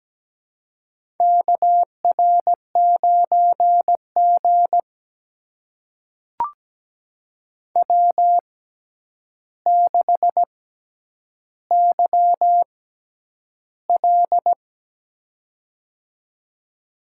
Morse code, heard as a keyed tone, about 17 wpm, spaced as for 9 wpm, 700 Hz. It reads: KR9G W6YL
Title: USA Call Signs - 4 Characters - 6x Character Spacing - 17wpm